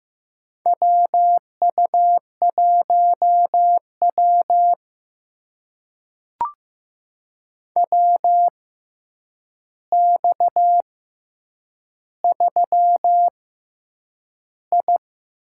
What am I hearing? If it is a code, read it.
Morse code, heard as a keyed tone, about 15 wpm, spaced as for 8 wpm, 700 Hz. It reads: WU1W WX3I
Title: USA Call Signs - 4 Characters - 6x Character Spacing - 15wpm